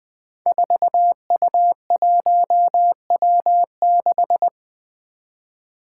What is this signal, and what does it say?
Morse code, heard as a keyed tone, about 20 wpm, 700 Hz. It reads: 4U1W6
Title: Five Challenging Letter-Number Combinations 20wpm